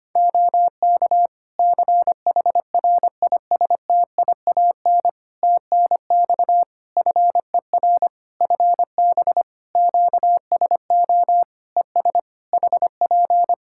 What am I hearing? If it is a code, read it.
Morse code, heard as a keyed tone, about 25 wpm, 700 Hz. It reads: OK CHRISTIAN TNX FER FB QSO ES HP
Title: Practice QSO - Exchange 4 - Ending - 25wpm